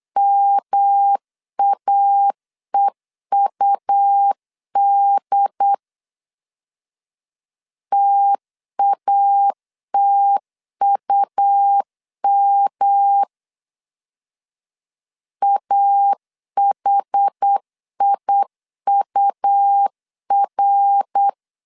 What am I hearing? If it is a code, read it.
Morse code, heard as a keyed tone, about 8 wpm, 800 Hz. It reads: MAEUD TATUM AHIUR
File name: CW / DET / 1 3 1500C S C 8wpm 4214